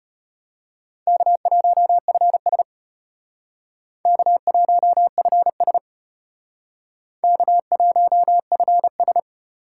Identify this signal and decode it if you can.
Morse code, heard as a keyed tone, about 32 wpm, 700 Hz. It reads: K1FS K1FS K1FS